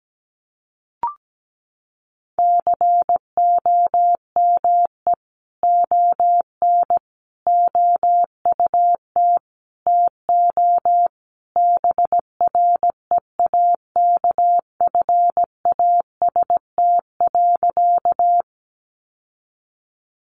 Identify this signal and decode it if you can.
Morse code, heard as a keyed tone, about 17 wpm, 700 Hz. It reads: COME ON OUT TO BREAKFAST.
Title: Sentences from Top 1500 Words 17wpm